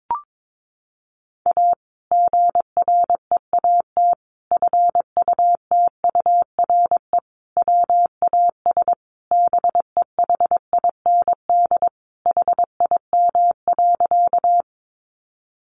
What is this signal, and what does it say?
Morse code, heard as a keyed tone, about 22 wpm, 700 Hz. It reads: A GREAT FUTURE WAS BEHIND HIM.
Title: Sentences from Top 700 Words 22wpm